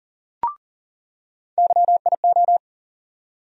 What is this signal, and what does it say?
Morse code, heard as a keyed tone, about 40 wpm, 700 Hz. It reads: YIO
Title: USA Call Sign Postfixes - 3 Letters 40wpm